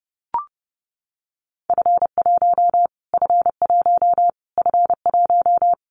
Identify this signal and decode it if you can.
Morse code, heard as a keyed tone, about 30 wpm, 700 Hz. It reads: F1 F1 F1
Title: Callsign Prefixes for Top 25 DXCC - Repeated 3x - 30wpm